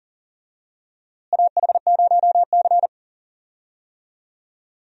Morse code, heard as a keyed tone, about 40 wpm, 700 Hz. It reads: AH0C
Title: USA Call Signs - 4 Characters - Repeated 2x - 40wpm